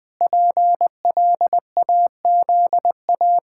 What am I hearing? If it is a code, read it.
Morse code, heard as a keyed tone, about 19 wpm, 700 Hz. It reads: PLAZA
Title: Pangram Challenge v2 - 20wpm